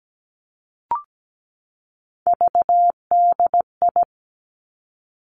Morse code, heard as a keyed tone, about 17 wpm, 700 Hz. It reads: VDI
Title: Three Challenging Letter-Number Combinations 17wpm